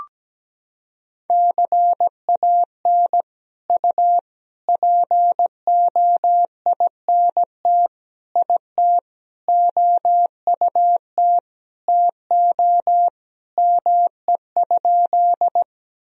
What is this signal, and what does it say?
Morse code, heard as a keyed tone, about 17 wpm, 700 Hz. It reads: CAN U POINT IT OUT TO ME?
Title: Sentences from Top 200 Words and Abbreviations 17wpm